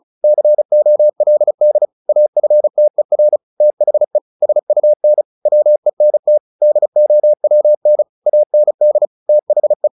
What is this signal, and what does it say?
Morse code, heard as a keyed tone, about 34 wpm, 600 Hz. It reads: COLD AFTER THE SUN WENT DOWN AND THE
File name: the old man and the sea / the old man and the sea 1 35wpm